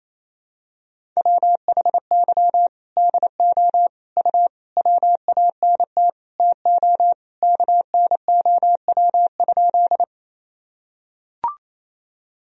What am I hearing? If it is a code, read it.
Morse code, heard as a keyed tone, about 28 wpm, 700 Hz. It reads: WHY DO U WANT TO KNOW?
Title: Sentences from Top 200 Words and Abbreviations 28wpm